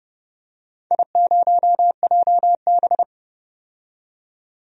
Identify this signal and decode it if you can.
Morse code, heard as a keyed tone, about 31 wpm, 700 Hz. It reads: I0JB
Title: DXCC-100 Call Signs - 4 Characters - 30wpm